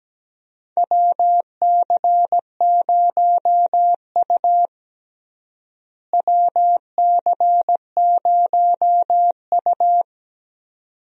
Morse code, heard as a keyed tone, about 17 wpm, 700 Hz. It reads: WC0U WC0U
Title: USA Call Signs - 4 Characters - Repeated 2x - 17wpm